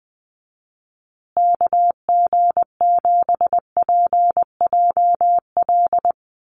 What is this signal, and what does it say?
Morse code, heard as a keyed tone, about 20 wpm, 700 Hz. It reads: KG7PJL
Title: USA Call Signs Repeated 3x 20wpm